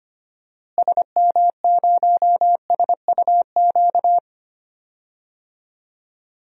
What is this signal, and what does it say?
Morse code, heard as a keyed tone, about 25 wpm, 700 Hz. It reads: SM0SUQ
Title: DXCC-100 Call Signs - 6 Characters - Repeated 3x - 25wpm